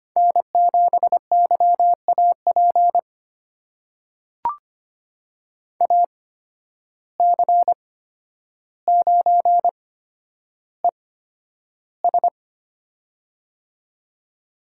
Morse code, heard as a keyed tone, about 25 wpm, spaced as for 11 wpm, 700 Hz. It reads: N7YAP AC9ES
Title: USA Call Signs - 5 Characters - 8x Character Spacing - 25wpm